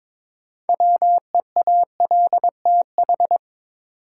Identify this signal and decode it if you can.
Morse code, heard as a keyed tone, about 21 wpm, 700 Hz. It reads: WEALTH
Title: Sentences from Top 1200 Words - New Word Drill Alt - 22wpm